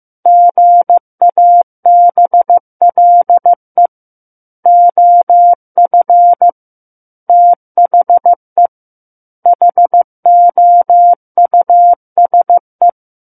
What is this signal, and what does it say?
Morse code, heard as a keyed tone, about 15 wpm, 700 Hz. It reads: GABLE OF THE HOUSE
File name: Chapter 6 - 15 WPM